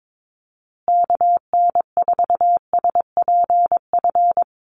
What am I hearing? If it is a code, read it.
Morse code, heard as a keyed tone, about 22 wpm, 700 Hz. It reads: KN4SPF
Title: USA Call Signs Repeated 3x 22wpm